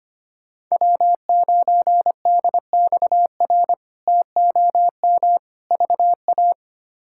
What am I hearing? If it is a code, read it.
Morse code, heard as a keyed tone, about 25 wpm, 700 Hz. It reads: W9DXR TOM VA
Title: K1USN SST QSO Practice - 25wpm